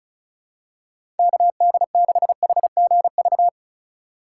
Morse code, heard as a keyed tone, about 34 wpm, 700 Hz. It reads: KD6HGV
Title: USA Call Signs - 6 Characters - 4x Character Spacing - 35wpm